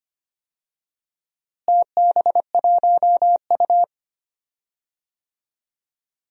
Morse code, heard as a keyed tone, about 25 wpm, 700 Hz. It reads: TB1U
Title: DXCC-100 Call Signs - 4 Characters - Repeated 2x - 25wpm